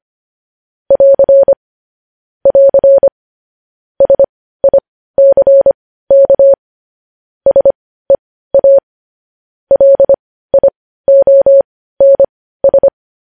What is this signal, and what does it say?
Morse code, heard as a keyed tone, about 25 wpm, spaced as for 15 wpm, 550 Hz. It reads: <AR> <AR> SICK SEA LIONS
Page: news-headlines-in-morse-code-at-15-wpm